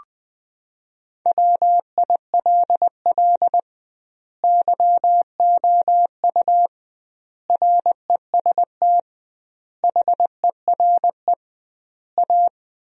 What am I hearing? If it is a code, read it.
Morse code, heard as a keyed tone, about 20 wpm, 700 Hz. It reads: WILL YOU REST HERE A
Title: 2x Word-Spacing - Sentences from Top 500 Words - 20wpm